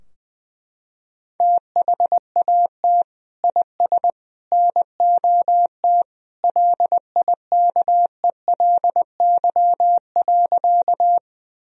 Morse code, heard as a keyed tone, about 20 wpm, 700 Hz. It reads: THAT IS NOT LIKELY.
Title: Sentences from Top 1000 Words - Review Part 4 - 20wpm